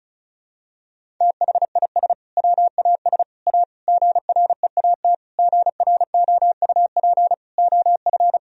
Morse code, heard as a keyed tone, about 34 wpm, 700 Hz. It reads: THIS WAS A GREAT GROUP OF
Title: Sentences from Top 1000 Words - Review Part 1 - 35wpm